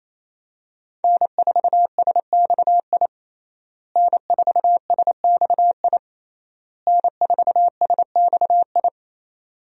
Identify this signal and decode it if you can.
Morse code, heard as a keyed tone, about 28 wpm, 700 Hz. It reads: N4SXI N4SXI N4SXI
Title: USA Call Signs - 5 Characters - Repeated 3x - 28wpm